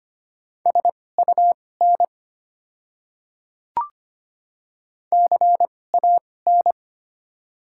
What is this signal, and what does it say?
Morse code, heard as a keyed tone, about 25 wpm, spaced as for 18 wpm, 700 Hz. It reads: SUN CAN
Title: 2x Character Spacing - Top 500 Words - 25wpm